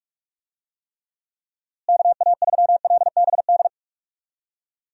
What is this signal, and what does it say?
Morse code, heard as a keyed tone, about 46 wpm, 700 Hz. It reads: KA3LBD
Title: USA Call Signs - 6 Characters - Repeated 2x - 45wpm